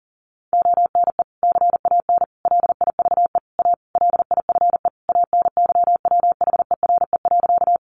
Morse code, heard as a keyed tone, about 40 wpm, 700 Hz. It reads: ONE CAN LIVE A LIFE ANYWHERE.
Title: Sentences from Top 1400 Words 40wpm